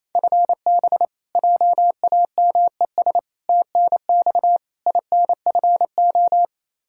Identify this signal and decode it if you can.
Morse code, heard as a keyed tone, about 27 wpm, 700 Hz. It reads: FB JAMES TNX INFO
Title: Practice QSO - Exchange 3 - More Info - 28wpm